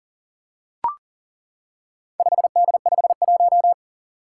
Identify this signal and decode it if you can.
Morse code, heard as a keyed tone, about 40 wpm, 700 Hz. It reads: 5D51
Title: Four Challenging Letter-Number Combinations 40wpm